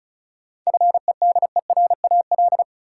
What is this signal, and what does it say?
Morse code, heard as a keyed tone, about 36 wpm, 700 Hz. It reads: FEDERAL
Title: Sentences from Top 1100 Words - New Word Drill Alt - 35wpm